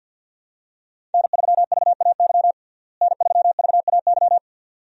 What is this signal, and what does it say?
Morse code, heard as a keyed tone, about 50 wpm, 700 Hz. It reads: N3VAY N3VAY
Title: USA Call Signs - 5 Characters - Repeated 2x - 50wpm